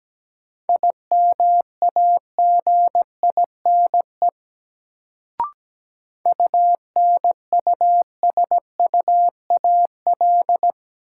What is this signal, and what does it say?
Morse code, heard as a keyed tone, about 17 wpm, 700 Hz. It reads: IMAGINE UNUSUAL
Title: Sentences from Top 1400 Words - New Word Drill Alt - 17wpm